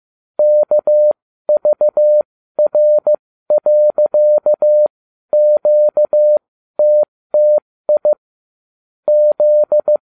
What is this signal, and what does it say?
Morse code, heard as a keyed tone, about 15 wpm, spaced as for 12 wpm, 600 Hz. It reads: KVR.QTTI Z